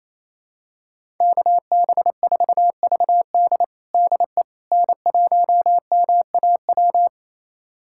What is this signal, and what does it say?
Morse code, heard as a keyed tone, about 28 wpm, 700 Hz. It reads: KB4VD DE N1MAW
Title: USA Callsign DE Callsign - 5 Characters - 28wpm